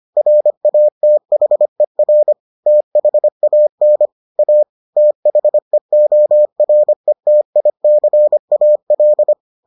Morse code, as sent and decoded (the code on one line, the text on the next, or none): .-. .- - .... .-. / - .... .- -. / .- / - .... --- .-. - .. -.-. .- .-..
RATHER THAN A THEORETICAL